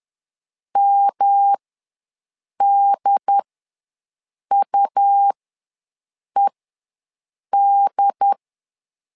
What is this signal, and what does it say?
Morse code, heard as a keyed tone, about 10 wpm, spaced as for 6 wpm, 800 Hz. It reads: MDUED
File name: LEZIONI / 3 MWRO / 3 M ws 10wpm 2121